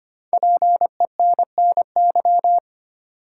Morse code, heard as a keyed tone, about 25 wpm, 700 Hz. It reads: PENNY
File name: 5 Letter Words - First 3 Letters Given - 25wpm